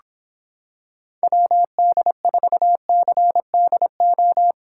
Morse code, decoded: WD4CDO

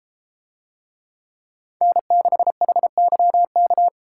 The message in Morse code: N6HYK

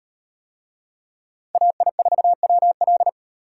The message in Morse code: AI4WL

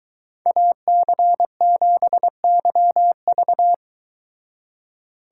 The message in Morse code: AC7YV

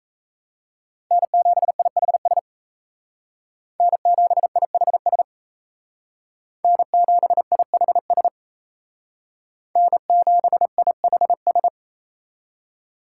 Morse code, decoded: N7IHS N7IHS N7IHS N7IHS

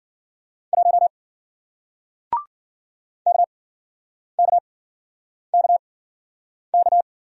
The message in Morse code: 1 KKKK

